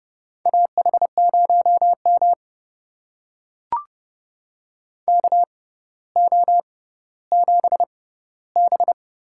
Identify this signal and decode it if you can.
Morse code, heard as a keyed tone, about 30 wpm, spaced as for 15 wpm, 700 Hz. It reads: AH0M KO7B
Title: USA Call Signs - 4 Characters - 6x Character Spacing - 30wpm